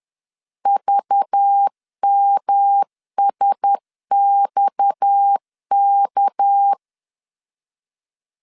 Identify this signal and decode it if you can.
Morse code, heard as a keyed tone, about 10 wpm, 800 Hz. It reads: VMSXK
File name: LEZIONI / 6 PKX0 / 6 PKX0 10wpm 2707